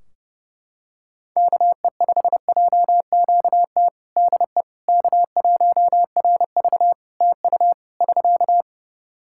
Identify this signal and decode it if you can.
Morse code, heard as a keyed tone, about 32 wpm, 700 Hz. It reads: KE5JQT DE K1RV TU <SK>
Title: Practice QSO - Exchange 4 - Ending - 30wpm